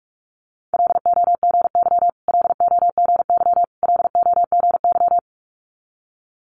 Morse code, heard as a keyed tone, about 46 wpm, 700 Hz. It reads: LOGY LOGY LOGY
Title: Multi-letter Phonemes Repeated 3x 45wpm